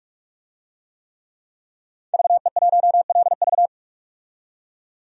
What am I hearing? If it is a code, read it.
Morse code, heard as a keyed tone, about 42 wpm, 700 Hz. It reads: VE1LV
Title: DXCC-100 Call Signs - 5 Characters - 45wpm